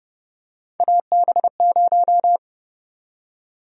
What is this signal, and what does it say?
Morse code, heard as a keyed tone, about 31 wpm, 700 Hz. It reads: AB0
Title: USA Call Sign Prefixes - Repeated 2x - 30wpm